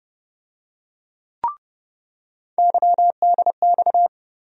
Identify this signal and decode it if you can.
Morse code, heard as a keyed tone, about 32 wpm, 700 Hz. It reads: YDX
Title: USA Call Sign Postfixes - 3 Letters 30wpm